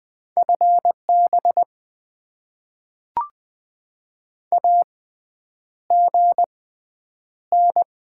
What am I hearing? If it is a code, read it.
Morse code, heard as a keyed tone, about 20 wpm, spaced as for 9 wpm, 700 Hz. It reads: FB AGN
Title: Top 20 QSO Elements - 6x Character Spacing - 20wpm